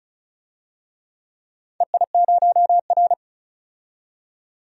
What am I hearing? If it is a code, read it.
Morse code, heard as a keyed tone, about 36 wpm, 700 Hz. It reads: EI0R
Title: DXCC-100 Call Signs - 4 Characters - 35wpm